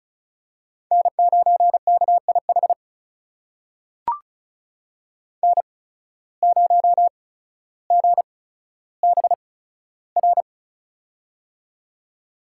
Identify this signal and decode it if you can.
Morse code, heard as a keyed tone, about 36 wpm, spaced as for 15 wpm, 700 Hz. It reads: N9KIH N0GBR
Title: USA Call Signs - 5 Characters - 8x Character Spacing - 35wpm